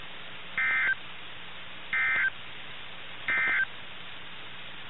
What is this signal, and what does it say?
Emergency Alert System (SAME) tones, heard as harsh buzzing bursts. It is heard from 0.5 to 3.7 s.
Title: NOAA Weather Radio - KIH61